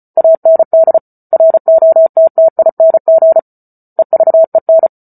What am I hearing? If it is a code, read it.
Morse code, heard as a keyed tone, about 34 wpm, 650 Hz. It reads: AND ROTTING EVEN
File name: Book Two / 08-Chapter 8 - 34 WPM